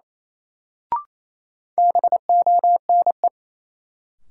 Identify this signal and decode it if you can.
Morse code, heard as a keyed tone, about 28 wpm, 700 Hz. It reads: BONE
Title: Sentences from Top 1600 Words - New Word Drill Alt - 28wpm